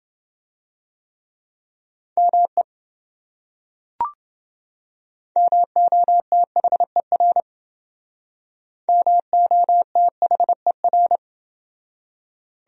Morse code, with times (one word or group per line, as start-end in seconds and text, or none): 2.2-2.6 ME
5.4-7.4 MOTHER
8.9-11.2 MOTHER